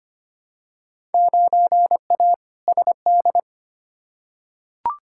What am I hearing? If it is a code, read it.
Morse code, heard as a keyed tone, about 25 wpm, 700 Hz. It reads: 9A SD
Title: Field Day Exchanges Encore - 25wpm